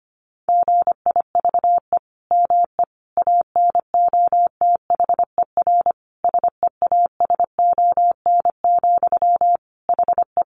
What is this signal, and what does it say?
Morse code, heard as a keyed tone, about 25 wpm, 700 Hz. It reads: GIVE ME ANOTHER SEASON, HE
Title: Sentences from Top 1400 Words 25wpm